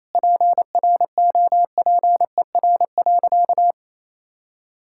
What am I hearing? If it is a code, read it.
Morse code, heard as a keyed tone, about 28 wpm, 700 Hz. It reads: PROPER.
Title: Sentences from Top 1000 Words - Review Part 5 - 28wpm